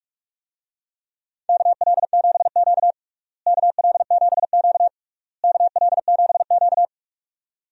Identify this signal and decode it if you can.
Morse code, heard as a keyed tone, about 42 wpm, 700 Hz. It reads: KL7Q KL7Q KL7Q
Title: DXCC-100 Call Signs - 4 Characters - Repeated 3x - 45wpm